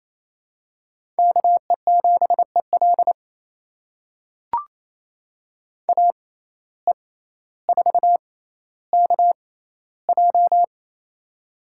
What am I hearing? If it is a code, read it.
Morse code, heard as a keyed tone, about 28 wpm, spaced as for 14 wpm, 700 Hz. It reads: KE7EL AE4KJ
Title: USA Call Signs - 5 Characters - 6x Character Spacing - 28wpm